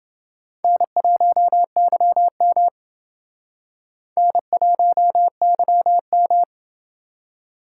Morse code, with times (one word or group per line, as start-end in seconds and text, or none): 0.6-2.7 N1YM
4.2-6.4 N1YM